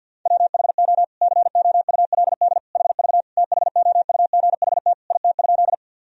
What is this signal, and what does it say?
Morse code, heard as a keyed tone, about 50 wpm, 700 Hz. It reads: WHO COULD HV THOUGHT IT?
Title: Sentences from Top 300 Words and Abbreviations 50wpm